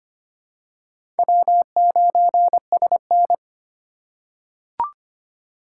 Morse code, heard as a keyed tone, about 25 wpm, 700 Hz. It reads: W9SN
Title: USA Call Signs - 4 Characters - 2x Character Spacing - 25wpm